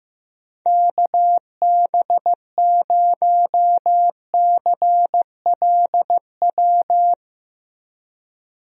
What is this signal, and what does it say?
Morse code, heard as a keyed tone, about 15 wpm, 700 Hz. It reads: KB0CLW
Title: USA Call Signs - 6 Characters - 15wpm